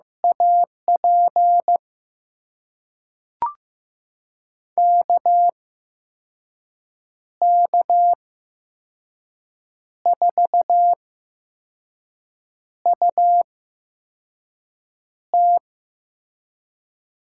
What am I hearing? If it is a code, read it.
Morse code, heard as a keyed tone, about 15 wpm, spaced as for 5 wpm, 700 Hz. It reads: AP KK4UT
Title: USA Call Signs - 6 Characters - 8x Character Spacing - 15wpm